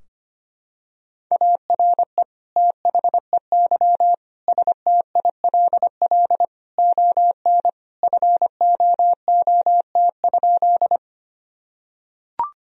Morse code, heard as a keyed tone, about 25 wpm, 700 Hz. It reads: ARE THEY STILL ON FOOT?
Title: Sentences from Top 1000 Words - Review Part 3 - 25wpm